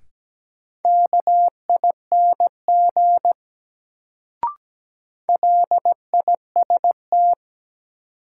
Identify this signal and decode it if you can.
Morse code, heard as a keyed tone, about 17 wpm, 700 Hz. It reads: KING LIST
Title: Sentences from Top 400 Words - New Word Drill - 17wpm